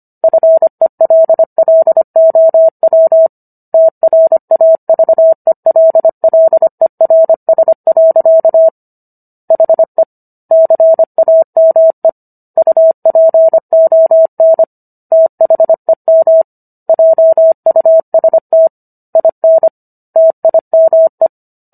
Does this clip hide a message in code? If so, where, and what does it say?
Yes: Morse code from 0.2 to 21.3 s, heard as a keyed tone, about 25 wpm, 650 Hz. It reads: FELLOW TRAVELLERS. HE CAME UPON THEM JUST IN TIME